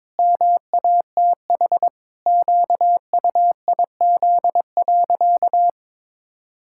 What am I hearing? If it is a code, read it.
Morse code, heard as a keyed tone, about 22 wpm, 700 Hz. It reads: MATH QUIZ.
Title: Pangram Challenge v2 - 22wpm